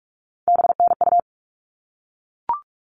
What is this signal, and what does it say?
Morse code, heard as a keyed tone, about 44 wpm, 700 Hz. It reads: BNU
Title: All Cut Numbers - 3-Digit 45wpm